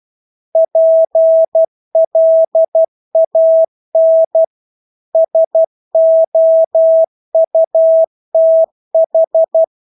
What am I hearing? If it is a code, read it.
Morse code, heard as a keyed tone, about 12 wpm, 650 Hz. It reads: PLAN SOUTH
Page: Engelske ord 12 wpm